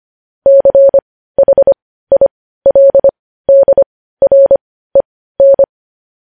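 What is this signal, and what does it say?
Morse code, heard as a keyed tone, about 25 wpm, spaced as for 15 wpm, 550 Hz. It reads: CHILDREN
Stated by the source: Morse code news headlines sent at 15.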